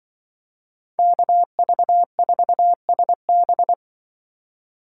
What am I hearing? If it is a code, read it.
Morse code, heard as a keyed tone, about 25 wpm, 700 Hz. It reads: KV4SB
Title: Speed Racing - USA 2x2 Call Signs 20wpm